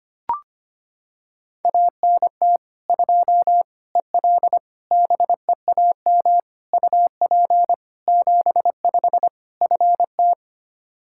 Morse code, heard as a keyed tone, about 25 wpm, 700 Hz. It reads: ANT 2 EL BEAM UP 75 FT
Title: Practice QSO - Exchange 2 - Info - 25wpm